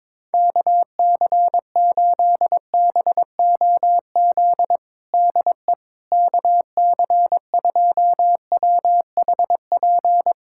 Morse code, heard as a keyed tone, about 21 wpm, 700 Hz. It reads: KC8BOZ DE KC2WHP
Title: USA Callsign DE Callsign - 6 Characters - 22wpm